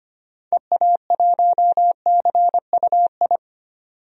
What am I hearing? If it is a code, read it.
Morse code, heard as a keyed tone, about 25 wpm, 700 Hz. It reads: EA1CUI
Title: DXCC-100 Call Signs - 6 Characters - 25wpm